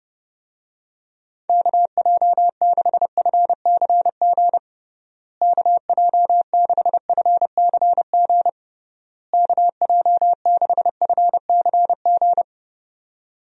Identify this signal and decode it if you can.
Morse code, heard as a keyed tone, about 30 wpm, 700 Hz. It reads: KJ6FCG KJ6FCG KJ6FCG
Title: USA Call Signs - 6 Characters - Repeated 3x - 30wpm